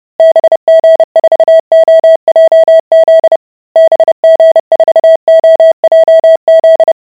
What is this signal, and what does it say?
Morse code, heard as a keyed tone, about 30 wpm, 650 Hz. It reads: BG4OJZ BG4OJZ